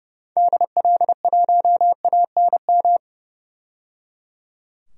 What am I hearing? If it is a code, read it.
Morse code, heard as a keyed tone, about 30 wpm, 700 Hz. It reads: DL1ANM
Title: DXCC-100 Call Signs - 6 Characters - 30wpm